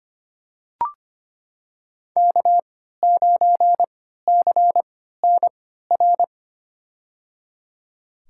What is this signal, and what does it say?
Morse code, heard as a keyed tone, about 25 wpm, spaced as for 14 wpm, 700 Hz. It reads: K9CNR
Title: USA Call Signs - 5 Characters - 3x Character Spacing - 25wpm